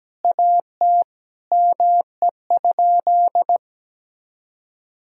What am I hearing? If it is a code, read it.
Morse code, heard as a keyed tone, about 17 wpm, 700 Hz. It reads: AT ME?
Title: Sentences from Top 1000 Words - Review Part 2 - 17wpm